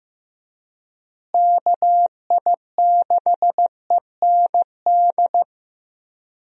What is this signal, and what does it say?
Morse code, heard as a keyed tone, about 15 wpm, 700 Hz. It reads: KI6END
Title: USA Call Signs - 6 Characters - Repeated 3x - 15wpm